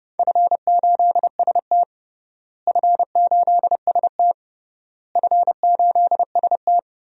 Morse code, heard as a keyed tone, about 30 wpm, 700 Hz. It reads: F8ST F8ST F8ST
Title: DXCC-100 Call Signs - 4 Characters - Repeated 3x - 30wpm